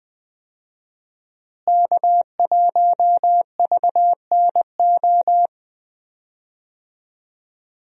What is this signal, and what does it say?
Morse code, heard as a keyed tone, about 20 wpm, 700 Hz. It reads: K1VNO
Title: USA Call Signs - 5 Characters - Repeated 2x - 20wpm